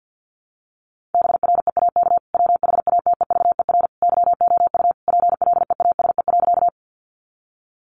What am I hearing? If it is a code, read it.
Morse code, heard as a keyed tone, about 50 wpm, 700 Hz. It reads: BREAK WHATEVER YOU PLEASE.
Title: Sentences from Top 1500 Words 50wpm